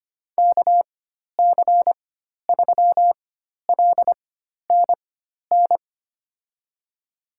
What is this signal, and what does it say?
Morse code, heard as a keyed tone, about 25 wpm, spaced as for 12 wpm, 700 Hz. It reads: KC3LNN
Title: USA Call Signs - 6 Characters - 4x Character Spacing - 25wpm